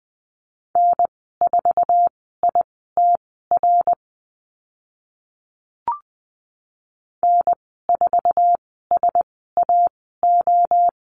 Morse code, heard as a keyed tone, about 20 wpm, spaced as for 14 wpm, 700 Hz. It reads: N4ITR N4SAO
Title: 062 Lesson 11g - Letter h - Call Signs 20wpm